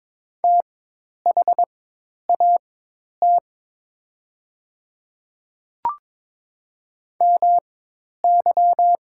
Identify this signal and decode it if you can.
Morse code, heard as a keyed tone, about 22 wpm, spaced as for 10 wpm, 700 Hz. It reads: THAT MY
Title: 4x Character Spacing - Top 100 Words - 22wpm